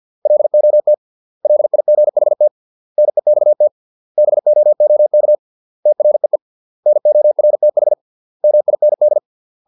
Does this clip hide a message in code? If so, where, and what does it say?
Yes: Morse code from 0.3 to 9.2 s, heard as a keyed tone, about 50 wpm, 600 Hz. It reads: LOT LIGHT NEXT BOOK TREE NORTH MIND